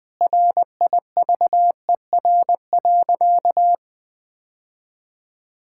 Morse code, heard as a keyed tone, about 20 wpm, 700 Hz. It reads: RIVER.
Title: Sentences with 5-Word Phrases - 20wpm